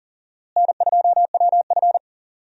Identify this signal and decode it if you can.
Morse code, heard as a keyed tone, about 39 wpm, 700 Hz. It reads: N2WF
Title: USA Call Signs - 4 Characters - Repeated 2x - 40wpm